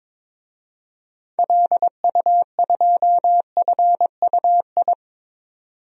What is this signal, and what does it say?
Morse code, heard as a keyed tone, about 22 wpm, 700 Hz. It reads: LU2FUI
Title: DXCC-100 Call Signs - 6 Characters - 22wpm